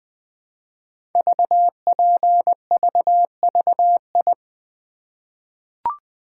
Morse code, heard as a keyed tone, about 20 wpm, 700 Hz. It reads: VPVVI
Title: Code Groups - 5 Letters - 20wpm